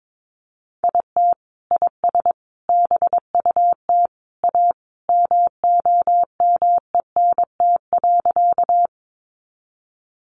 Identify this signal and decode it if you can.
Morse code, heard as a keyed tone, about 22 wpm, 700 Hz. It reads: IT IS BUT A MOMENT.
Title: Sentences from Top 1000 Words 22wpm